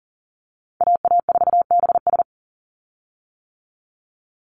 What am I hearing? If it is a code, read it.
Morse code, heard as a keyed tone, about 40 wpm, 700 Hz. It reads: AA4BS
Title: USA Call Signs 40wpm